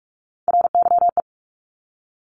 Morse code, heard as a keyed tone, about 46 wpm, 700 Hz. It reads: RYE
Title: Multi-letter Phonemes 45wpm